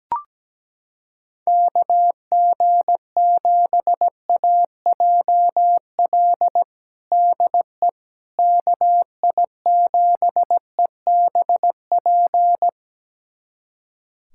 Morse code, heard as a keyed tone, about 17 wpm, 700 Hz. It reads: KG7AJL DE KI7EBP
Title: USA Callsign DE Callsign - 6 Characters - 17wpm